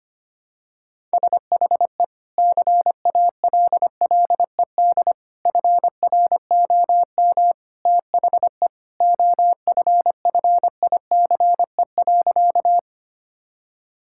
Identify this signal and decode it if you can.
Morse code, heard as a keyed tone, about 25 wpm, 700 Hz. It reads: SHE CALLED FROM THE OFFICE.
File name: practice / 25wpm / Sentences with 2-Word Phrases - 25wpm